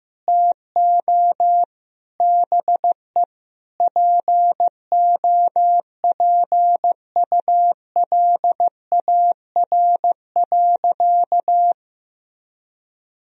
Morse code, heard as a keyed tone, about 15 wpm, 700 Hz. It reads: TO BE POPULAR.
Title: Sentences from Top 1000 Words - Review Part 3 - 15wpm